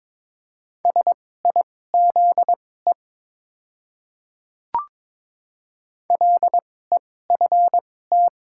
Morse code, heard as a keyed tone, about 22 wpm, spaced as for 16 wpm, 690 Hz. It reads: SIZE LEFT
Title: 2x Character Spacing - Top 500 Words - 22wpm